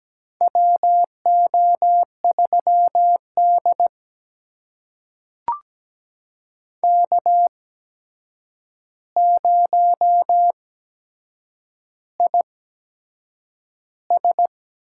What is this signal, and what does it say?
Morse code, heard as a keyed tone, about 17 wpm, spaced as for 7 wpm, 700 Hz. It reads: WO3D K0IS
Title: USA Call Signs - 4 Characters - 8x Character Spacing - 17wpm